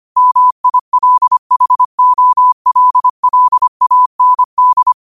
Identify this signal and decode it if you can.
Morse code, heard as a keyed tone, about 25 wpm, 1000 Hz. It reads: MILHOLLAND